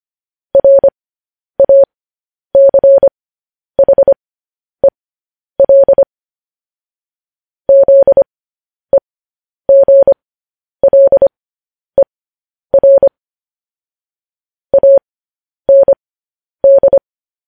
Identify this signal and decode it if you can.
Morse code, heard as a keyed tone, about 25 wpm, spaced as for 10 wpm, 550 Hz. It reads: RACHEL ZEGLER AND
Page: news-headlines-in-morse-code-at-10-wpm